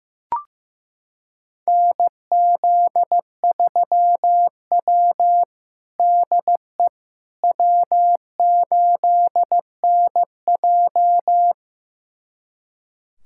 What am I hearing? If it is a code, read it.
Morse code, heard as a keyed tone, about 15 wpm, 700 Hz. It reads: NZ3W DE W8NJ
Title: USA Callsign DE Callsign - 4 Characters - 15wpm